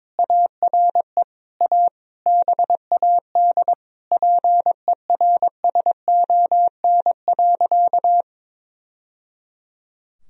Morse code, heard as a keyed tone, about 22 wpm, 700 Hz. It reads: ARE A BAD PERSON.
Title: Sentences from Top 1000 Words - Review Part 3 - 22wpm